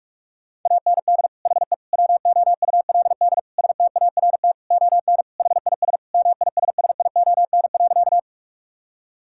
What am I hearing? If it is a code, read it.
Morse code, heard as a keyed tone, about 42 wpm, 700 Hz. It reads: AND HE WOULD START ON HIS MISSION.
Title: Sentences from Top 1600 Words 45wpm